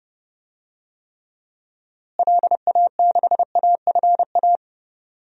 Morse code, heard as a keyed tone, about 32 wpm, 700 Hz. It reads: LA6AFA
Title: DXCC-100 Call Signs - 6 Characters - Repeated 2x - 30wpm